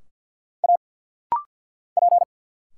Morse code, heard as a keyed tone, about 50 wpm, 700 Hz. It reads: A P
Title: Single Letters - Rapid-Fire 50wpm